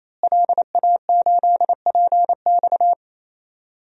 Morse code, heard as a keyed tone, about 28 wpm, 700 Hz. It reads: LA8PX
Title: DXCC-100 Call Signs - 5 Characters - Repeated 3x - 28wpm